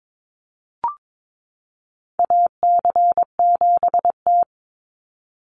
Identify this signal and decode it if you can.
Morse code, heard as a keyed tone, about 22 wpm, 700 Hz. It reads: AC7T
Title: Call Signs - 22wpm